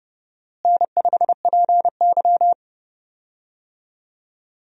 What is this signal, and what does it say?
Morse code, heard as a keyed tone, about 29 wpm, 700 Hz. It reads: N5PY
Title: USA Call Signs - 4 Characters - Repeated 3x - 30wpm